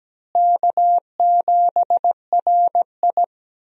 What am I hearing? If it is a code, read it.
Morse code, heard as a keyed tone, about 17 wpm, 700 Hz. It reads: K7RI
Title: USA Call Signs - 4 Characters - Repeated 3x - 17wpm